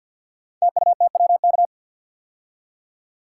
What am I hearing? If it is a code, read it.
Morse code, heard as a keyed tone, about 51 wpm, 700 Hz. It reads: TUTWK
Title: Code Groups - 5 Letters - 50wpm